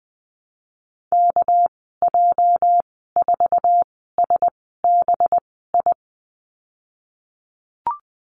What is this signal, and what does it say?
Morse code, heard as a keyed tone, about 20 wpm, spaced as for 14 wpm, 700 Hz. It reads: KJ4SBI
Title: 245 Lesson 39g - Letter z - Call Signs 20wpm